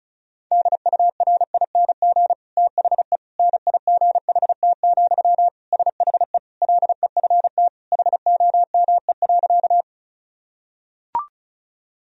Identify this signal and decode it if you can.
Morse code, heard as a keyed tone, about 36 wpm, 700 Hz. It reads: DURING THE NIGHT, SHE LEFT HOME.
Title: Sentences from Top 1000 Words - Review Part 2 - 35wpm